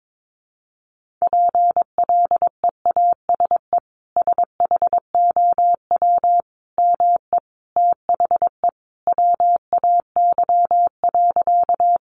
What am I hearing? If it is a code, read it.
Morse code, heard as a keyed tone, about 22 wpm, 700 Hz. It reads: PLEASE SHOW ME THE WAY.
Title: Sentences from Top 1500 Words 22wpm